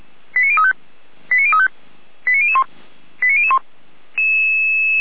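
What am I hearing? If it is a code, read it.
Five-tone selective call (ZVEI), heard as quick runs of five tones, tones of about 80 ms. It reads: 89035 89035 89001 89001
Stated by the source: ZVEI five-tone call received by SDR.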